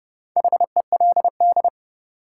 Morse code, heard as a keyed tone, about 32 wpm, 700 Hz. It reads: HELD
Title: Sentences from Top 1600 Words - New Word Drill - 30wpm